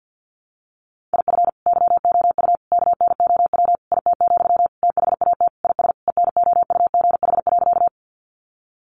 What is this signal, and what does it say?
Morse code, heard as a keyed tone, about 50 wpm, 700 Hz. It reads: IF YOU KNOW IT, THAT IS ENOUGH.
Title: Sentences from Top 400 Words 50wpm